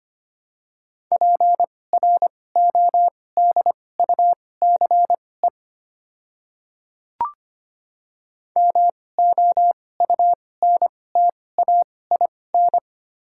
Morse code, heard as a keyed tone, about 25 wpm, spaced as for 18 wpm, 700 Hz. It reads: PRODUCE MOUNTAIN
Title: 2x Character Spacing - Top 500 Words - 25wpm